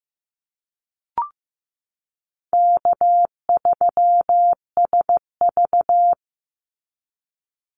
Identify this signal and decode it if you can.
Morse code, heard as a keyed tone, about 15 wpm, 700 Hz. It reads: K3SV
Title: Easy Call Signs - 15wpm